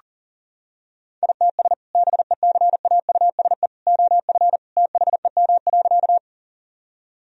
Morse code, heard as a keyed tone, about 40 wpm, 700 Hz. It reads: ITS BECAUSE OF THEM.
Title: Sentences from Top 1000 Words - Review Part 3 - 40wpm